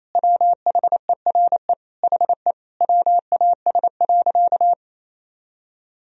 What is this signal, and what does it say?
Morse code, heard as a keyed tone, about 28 wpm, 700 Hz. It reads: WHERE HE WAS.
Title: Sentences from Top 1000 Words - Review Part 1 - 28wpm